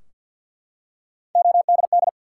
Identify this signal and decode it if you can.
Morse code, heard as a keyed tone, about 53 wpm, 700 Hz. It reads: ODD
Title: Three Letter Words - 50wpm